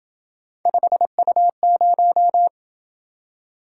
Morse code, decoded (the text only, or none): 5U0